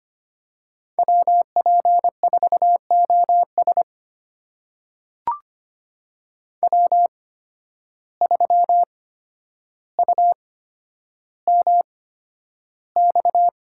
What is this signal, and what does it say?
Morse code, heard as a keyed tone, about 25 wpm, spaced as for 11 wpm, 700 Hz. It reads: WP4OS W3UMX
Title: USA Call Signs - 5 Characters - 8x Character Spacing - 25wpm